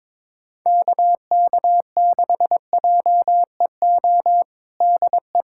Morse code, heard as a keyed tone, about 21 wpm, 700 Hz. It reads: KK6JEO DE
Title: USA Callsign DE Callsign 22wpm